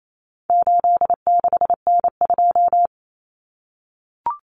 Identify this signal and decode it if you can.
Morse code, heard as a keyed tone, about 28 wpm, 700 Hz. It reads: 86N2
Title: Common Cut Numbers - 4-Digit 28wpm